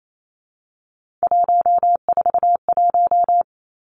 Morse code, heard as a keyed tone, about 28 wpm, 700 Hz. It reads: 141
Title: Three Challenging Letter-Number Combinations 28wpm